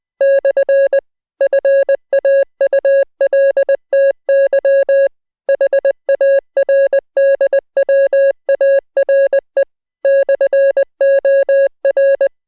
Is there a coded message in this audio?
Yes: Morse code from 0.2 to 12.3 s, heard as a keyed tone, about 20 wpm, 550 Hz. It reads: / FAULTY HARDWARE /OR